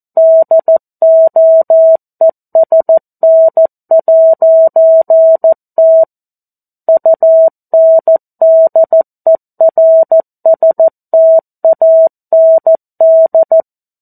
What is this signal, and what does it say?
Morse code, heard as a keyed tone, about 14 wpm, 650 Hz. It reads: DOESN'T UNDERSTAND
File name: 02-Chapter-02 14 WPM